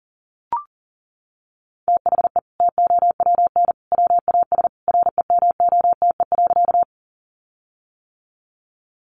Morse code, heard as a keyed tone, about 40 wpm, 700 Hz. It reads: THE TOWN WAS REMOTE.
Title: Sentences from Top 1300 Words 40wpm